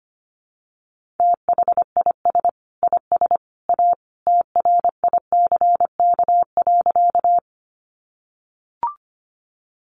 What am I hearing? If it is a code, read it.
Morse code, heard as a keyed tone, about 25 wpm, 700 Hz. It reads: THIS IS A TRICK.
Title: Sentences from Top 1500 Words 25wpm